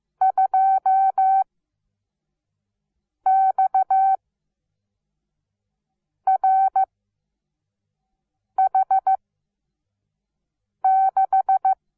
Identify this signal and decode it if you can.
Morse code, heard as a keyed tone, about 15 wpm, spaced as for 4 wpm, 770 Hz. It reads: 2XRH6
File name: Onlinelehrgang / morsen / DJ4UF Morsen17